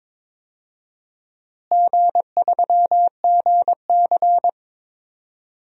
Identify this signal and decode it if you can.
Morse code, heard as a keyed tone, about 22 wpm, 700 Hz. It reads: G3GC
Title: DXCC-100 Call Signs - 4 Characters - Repeated 2x - 22wpm